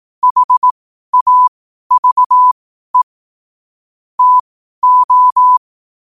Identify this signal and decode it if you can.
Morse code, heard as a keyed tone, about 18 wpm, spaced as for 12 wpm, 1000 Hz. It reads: HAVE TO
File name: qotdfiles / 12wpm